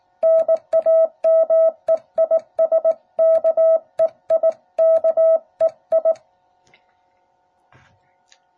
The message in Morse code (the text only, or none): NAME IS KEI KEI